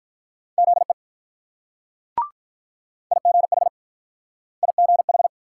Morse code, heard as a keyed tone, about 48 wpm, 700 Hz. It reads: ZE IGH IGH